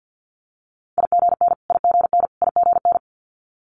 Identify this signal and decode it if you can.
Morse code, heard as a keyed tone, about 50 wpm, 690 Hz. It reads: IGN IGN IGN
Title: Multi-letter Phonemes Repeated 3x 50wpm